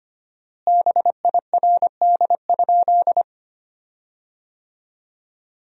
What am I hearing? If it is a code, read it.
Morse code, heard as a keyed tone, about 25 wpm, 700 Hz. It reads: BIRD?
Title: 4x Word-Spacing - Sentences from Top 400 Words - 25wpm